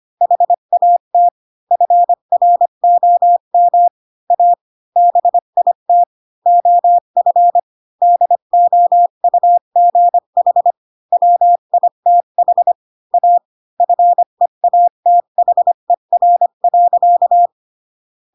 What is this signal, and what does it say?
Morse code, heard as a keyed tone, about 25 wpm, 700 Hz. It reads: HAT FROM A BIT OF DOUGH WITH A FEATHER.
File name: Pinocchio / Pinocchio ch8 20 25 ik1hge